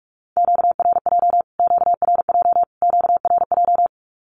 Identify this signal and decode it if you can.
Morse code, heard as a keyed tone, about 44 wpm, 700 Hz. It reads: QRJ QRJ QRJ
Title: Q-Codes Repeated 3x 45wpm